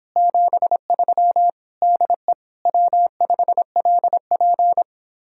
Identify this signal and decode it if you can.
Morse code, heard as a keyed tone, about 31 wpm, 700 Hz. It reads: 73 DE W5LP